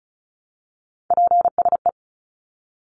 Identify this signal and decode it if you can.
Morse code, heard as a keyed tone, about 34 wpm, 700 Hz. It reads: PSE
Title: Abbreviations 35wpm